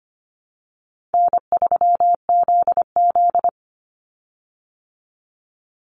Morse code, heard as a keyed tone, about 25 wpm, 700 Hz. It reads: N3ZZ
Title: Call Signs Repeated 3x - 25wpm